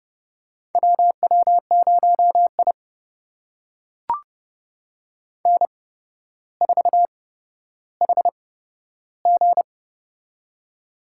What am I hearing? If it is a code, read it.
Morse code, heard as a keyed tone, about 30 wpm, spaced as for 13 wpm, 700 Hz. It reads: WW0I N4HG
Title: USA Call Signs - 4 Characters - 8x Character Spacing - 30wpm